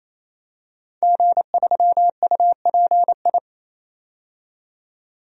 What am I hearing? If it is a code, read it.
Morse code, heard as a keyed tone, about 28 wpm, 700 Hz. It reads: G3UPI
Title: DXCC-100 Call Signs - 5 Characters - 28wpm